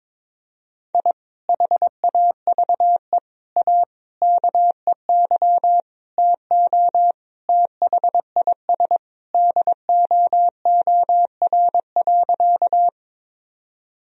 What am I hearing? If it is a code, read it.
Morse code, heard as a keyed tone, about 22 wpm, 700 Hz. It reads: I HAVE A KEY TO THIS DOOR.
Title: Sentences from Top 1000 Words - Review Part 4 - 22wpm